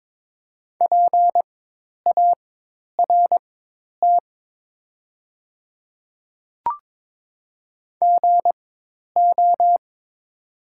Morse code, heard as a keyed tone, about 22 wpm, spaced as for 10 wpm, 700 Hz. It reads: PART GO
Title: 4x Character Spacing - Top 500 Words - 22wpm